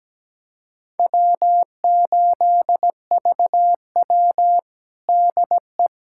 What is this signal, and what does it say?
Morse code, heard as a keyed tone, about 17 wpm, 700 Hz. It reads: W8VW DE
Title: USA Callsign DE Callsign - 4 Characters - 17wpm